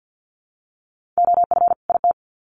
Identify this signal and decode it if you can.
Morse code, heard as a keyed tone, about 50 wpm, 700 Hz. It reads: OF IT
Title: Sets of 2 Words 50wpm